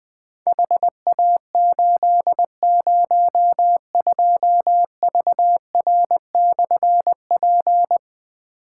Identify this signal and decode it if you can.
Morse code, heard as a keyed tone, about 20 wpm, 700 Hz. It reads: HA802VR/P
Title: Difficult Call Signs 20wpm